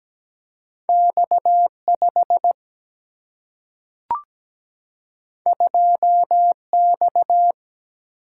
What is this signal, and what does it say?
Morse code, heard as a keyed tone, about 17 wpm, 700 Hz. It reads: X5 2X
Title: Single-Letter Single-Number 17wpm